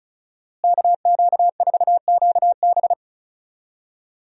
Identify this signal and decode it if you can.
Morse code, heard as a keyed tone, about 36 wpm, 700 Hz. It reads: KQ4QB
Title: USA Call Signs - 5 Characters - Repeated 2x - 35wpm